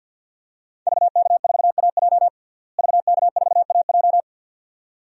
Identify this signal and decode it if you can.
Morse code, heard as a keyed tone, about 50 wpm, 700 Hz. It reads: VK4AJ VK4AJ
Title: DXCC-100 Call Signs - 5 Characters - Repeated 2x - 50wpm